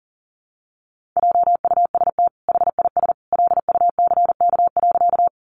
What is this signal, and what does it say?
Morse code, heard as a keyed tone, about 40 wpm, 700 Hz. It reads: JUST HIS LUCK.
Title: Sentences from Top 1500 Words 40wpm